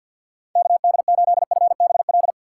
Morse code, heard as a keyed tone, about 51 wpm, 700 Hz. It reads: KD8FBL
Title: USA Call Signs - 6 Characters - 6x Character Spacing - 50wpm